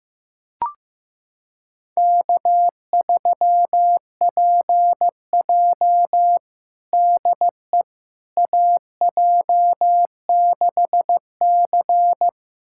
Morse code, heard as a keyed tone, about 16 wpm, 700 Hz. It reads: K3PJ DE AJ6C
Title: USA Callsign DE Callsign - 4 Characters - 15wpm